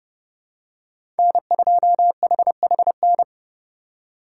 Morse code, heard as a keyed tone, about 29 wpm, 700 Hz. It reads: N2HHN